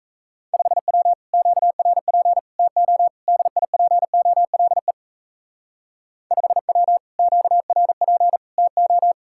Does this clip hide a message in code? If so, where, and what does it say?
Yes: Morse code from 0.5 to 9.1 s, heard as a keyed tone, about 35 wpm, 700 Hz. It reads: 5W QRP TO DIPOLE 5W QRP TO